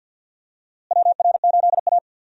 Morse code, heard as a keyed tone, about 52 wpm, 700 Hz. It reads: WR8A
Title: USA Call Signs - 4 Characters - Repeated 2x - 50wpm